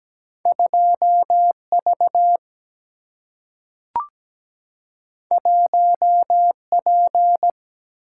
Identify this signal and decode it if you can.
Morse code, heard as a keyed tone, about 17 wpm, 700 Hz. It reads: 2 V 1 P
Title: Single-Letter Single-Number 17wpm